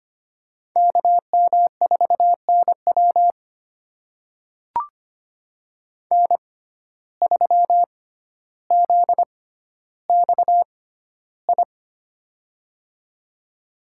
Morse code, heard as a keyed tone, about 25 wpm, spaced as for 13 wpm, 700 Hz. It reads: KM4NW N3ZXI
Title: USA Call Signs - 5 Characters - 6x Character Spacing - 25wpm